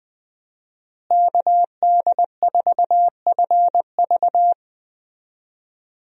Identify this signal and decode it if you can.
Morse code, heard as a keyed tone, about 20 wpm, 700 Hz. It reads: KD4FV
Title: USA Call Signs - 5 Characters - Repeated 2x - 20wpm